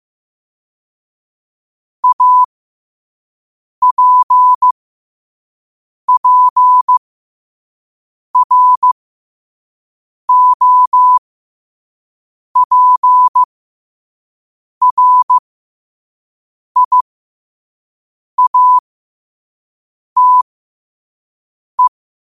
Morse code, heard as a keyed tone, about 15 wpm, spaced as for 5 wpm, 1000 Hz. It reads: APPROPRIATE
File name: qotdfiles / 5wpm